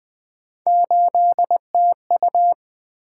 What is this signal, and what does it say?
Morse code, heard as a keyed tone, about 21 wpm, 700 Hz. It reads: 8TU
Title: Three Letter-Number Combinations - 20wpm